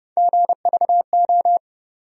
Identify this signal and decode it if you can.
Morse code, heard as a keyed tone, about 30 wpm, 700 Hz. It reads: GVO
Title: USA Call Sign Postfixes - 3 Letters 30wpm